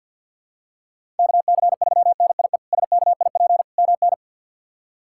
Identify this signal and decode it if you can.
Morse code, heard as a keyed tone, about 52 wpm, 690 Hz. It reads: KC3NIE SKIP MN
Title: K1USN SST QSO Practice - 50wpm